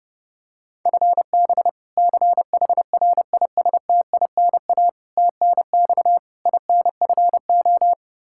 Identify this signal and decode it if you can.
Morse code, heard as a keyed tone, about 30 wpm, 700 Hz. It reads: FB CHRISTINA TNX INFO
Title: Practice QSO - Exchange 3 - More Info - 30wpm